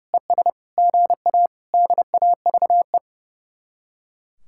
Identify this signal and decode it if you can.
Morse code, heard as a keyed tone, about 39 wpm, 700 Hz. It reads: ES GA DAVE